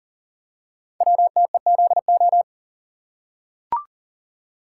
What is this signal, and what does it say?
Morse code, heard as a keyed tone, about 40 wpm, 700 Hz. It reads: WTEZO
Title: Code Groups - 5 Letters - 40wpm